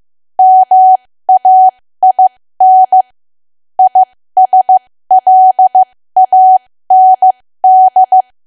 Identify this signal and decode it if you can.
Morse code, heard as a keyed tone, about 15 wpm, spaced as for 13 wpm, 750 Hz. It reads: MAIN ISLAND
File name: cwa / W1AW-Level2 / 13WPM